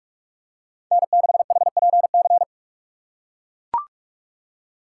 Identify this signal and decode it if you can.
Morse code, heard as a keyed tone, about 42 wpm, 700 Hz. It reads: N6HPC
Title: USA Call Signs - 5 Characters - 2x Character Spacing - 45wpm